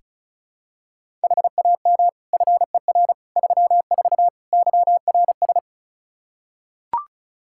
Morse code, heard as a keyed tone, about 36 wpm, 700 Hz. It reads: HAM FER 34 YRS
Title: Practice QSO - Exchange 3 - More Info - 35wpm